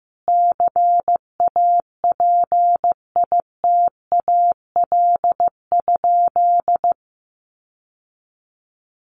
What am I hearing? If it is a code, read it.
Morse code, heard as a keyed tone, about 15 wpm, 700 Hz. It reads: CAPITAL?
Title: Sentences from Top 1100 Words 15wpm